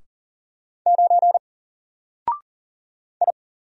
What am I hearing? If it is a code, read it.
Morse code, heard as a keyed tone, about 40 wpm, 700 Hz. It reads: 9 I
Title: Single Character (Letter:Number:Punctuation:Prosign) Alt 40wpm